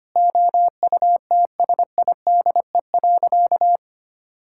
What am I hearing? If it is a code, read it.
Morse code, heard as a keyed tone, about 25 wpm, 700 Hz. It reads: OUTSIDE.
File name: practice / 25wpm / Sentences with 3-Word Phrases - 25wpm